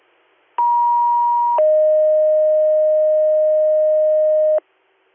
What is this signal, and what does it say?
Two-tone page, heard as a one-second tone, then a three-second tone, about 950 then 620 Hz. It is heard from 0.6 to 4.6 s.